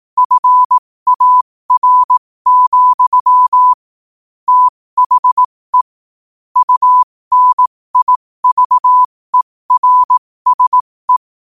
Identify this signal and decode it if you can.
Morse code, heard as a keyed tone, about 18 wpm, spaced as for 15 wpm, 1000 Hz. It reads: FAR, THE UNIVERSE
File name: qotdfiles / 15wpm